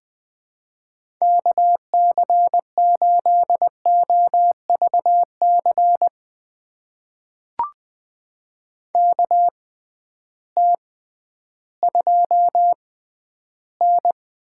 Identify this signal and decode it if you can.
Morse code, heard as a keyed tone, about 20 wpm, spaced as for 12 wpm, 700 Hz. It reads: KC8OVC KT2N
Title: USA Call Signs - 6 Characters - 6x Character Spacing - 20wpm